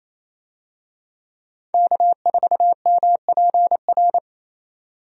Morse code, heard as a keyed tone, about 28 wpm, 700 Hz. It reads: K4MPR
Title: USA Call Signs - 5 Characters - 28wpm